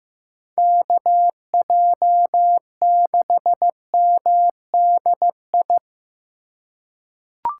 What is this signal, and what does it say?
Morse code, heard as a keyed tone, about 16 wpm, 700 Hz. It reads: KJ6MDI
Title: USA Call Signs - 6 Characters - 4x Character Spacing - 15wpm